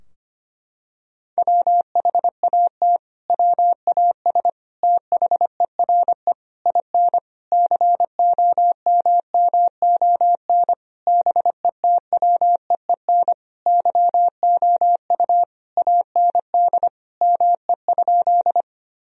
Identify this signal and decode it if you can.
Morse code, heard as a keyed tone, about 25 wpm, 700 Hz. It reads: WHAT WAS THERE IN COMMON BETWEEN YOU AND ME?
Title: Sentences from Top 1000 Words - Review Part 3 - 25wpm